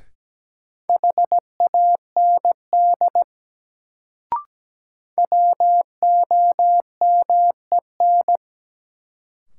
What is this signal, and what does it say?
Morse code, heard as a keyed tone, about 17 wpm, 700 Hz. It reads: HAND WOMEN
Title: Sets of 3 Words - New Word Drill - 17wpm